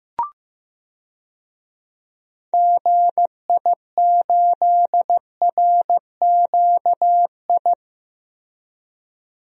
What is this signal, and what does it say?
Morse code, heard as a keyed tone, about 15 wpm, 700 Hz. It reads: GI8RQI
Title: DXCC-100 Call Signs - 6 Characters - Repeated 2x - 15wpm